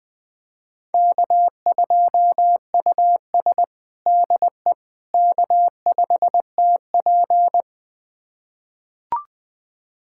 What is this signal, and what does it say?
Morse code, heard as a keyed tone, about 20 wpm, 700 Hz. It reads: K2US DE K5TP
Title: USA Callsign DE Callsign - 4 Characters - 20wpm